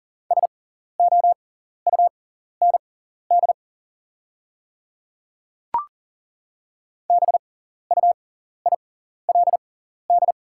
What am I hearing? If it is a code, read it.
Morse code, heard as a keyed tone, about 38 wpm, spaced as for 14 wpm, 690 Hz. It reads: SOUND BUILD
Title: 6x Character Spacing - Top 300 Words - 40wpm